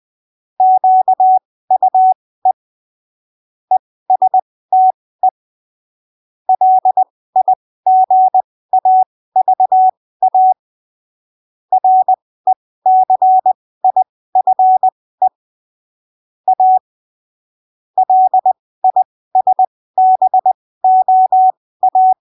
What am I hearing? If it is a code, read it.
Morse code, heard as a keyed tone, about 20 wpm, spaced as for 15 wpm, 750 Hz. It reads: QUE ESTE LIGAVA RECIFE A LISBOA
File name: arquivos / texto / texto 16 20ppm